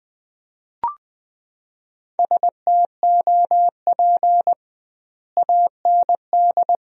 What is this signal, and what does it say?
Morse code, heard as a keyed tone, about 20 wpm, 700 Hz. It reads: STOP AND
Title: 2x Word-Spacing - Sentences from Top 300 Words - 20wpm